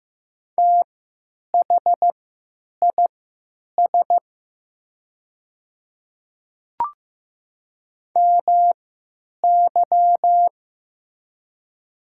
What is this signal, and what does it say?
Morse code, heard as a keyed tone, about 15 wpm, spaced as for 9 wpm, 700 Hz. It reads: THIS MY
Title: 3x Character Spacing - Top 200 Words - 15wpm